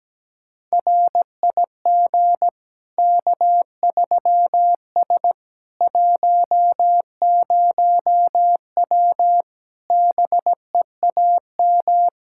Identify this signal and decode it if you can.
Morse code, heard as a keyed tone, about 17 wpm, 700 Hz. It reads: RIG K3S 10W BEAM
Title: Practice QSO - Exchange 2 - Info - 17wpm